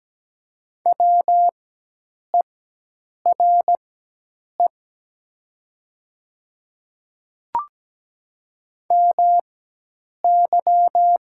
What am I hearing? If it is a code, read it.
Morse code, heard as a keyed tone, about 17 wpm, spaced as for 8 wpm, 700 Hz. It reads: WERE MY
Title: 4x Character Spacing - Top 100 Words - 17wpm